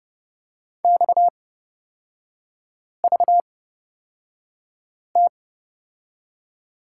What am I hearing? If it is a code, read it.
Morse code, heard as a keyed tone, about 29 wpm, spaced as for 5 wpm, 700 Hz. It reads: XVT
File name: Lesson 37e - Letter x - Three Characters 30wpm